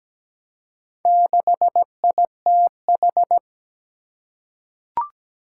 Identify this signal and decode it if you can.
Morse code, heard as a keyed tone, about 17 wpm, 700 Hz. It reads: 6ITH
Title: Four Challenging Letter-Number Combinations 17wpm